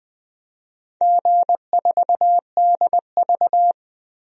Morse code, heard as a keyed tone, about 20 wpm, 700 Hz. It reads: G4DV
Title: DXCC-100 Call Signs - 4 Characters - 20wpm